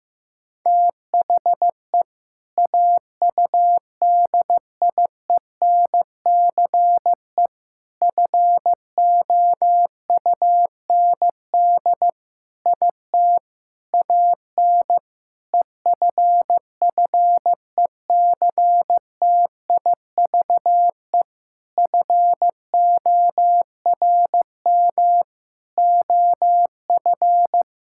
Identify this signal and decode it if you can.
Morse code, heard as a keyed tone, about 15 wpm, 700 Hz. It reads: THE AUDIENCE FOUND IT AN EFFECTIVE FORM OF
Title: Sentences from Top 1000 Words - Review Part 6 - 15wpm